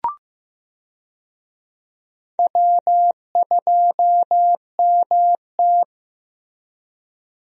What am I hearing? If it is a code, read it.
Morse code, heard as a keyed tone, about 15 wpm, 700 Hz. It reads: W2MT
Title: USA Call Signs - 4 Characters - Repeated 2x - 15wpm